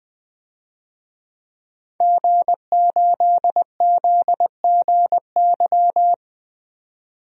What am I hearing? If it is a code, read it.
Morse code, heard as a keyed tone, about 20 wpm, 700 Hz. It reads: G8ZGY
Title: DXCC-100 Call Signs - 5 Characters - 20wpm